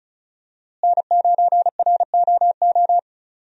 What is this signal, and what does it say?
Morse code, heard as a keyed tone, about 36 wpm, 700 Hz. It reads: N9ROO
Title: USA Call Signs - 5 Characters - Repeated 3x - 35wpm